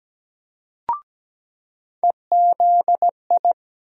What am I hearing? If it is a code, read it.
Morse code, heard as a keyed tone, about 17 wpm, 700 Hz. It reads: EZI
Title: USA Call Sign Postfixes - 3 Letters 17wpm